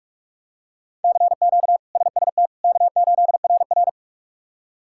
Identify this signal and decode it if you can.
Morse code, heard as a keyed tone, about 48 wpm, 700 Hz. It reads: CQ SST K8RR
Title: K1USN SST QSO Practice - 45wpm